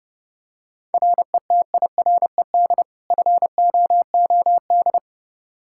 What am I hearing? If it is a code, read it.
Morse code, heard as a keyed tone, about 32 wpm, 700 Hz. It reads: RETIRED FOOD
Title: Practice QSO - Exchange 3 - More Info - 30wpm